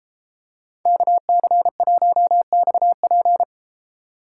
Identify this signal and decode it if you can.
Morse code, heard as a keyed tone, about 33 wpm, 700 Hz. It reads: KC1XP